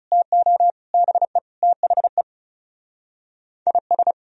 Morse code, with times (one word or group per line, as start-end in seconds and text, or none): 0.1-0.7 TO
0.9-1.4 BE
1.6-2.2 THE
3.7-4.1 IS